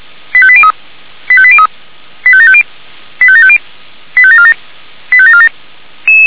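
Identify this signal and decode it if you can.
Five-tone selective call (ZVEI), heard as quick runs of five tones, tones of about 70 ms. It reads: 85803 85803 85750 85750 85747 85747
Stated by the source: ZVEI five-tone call received by SDR.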